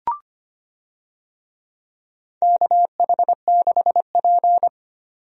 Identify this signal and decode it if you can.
Morse code, heard as a keyed tone, about 25 wpm, 700 Hz. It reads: KH6P
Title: DXCC-100 Call Signs - 4 Characters - 25wpm